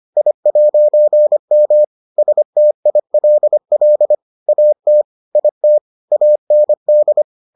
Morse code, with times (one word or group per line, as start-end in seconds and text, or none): 0.2-1.8 I'M
2.2-4.2 STILL
4.5-5.0 AT
5.4-5.8 IT
6.1-7.2 AND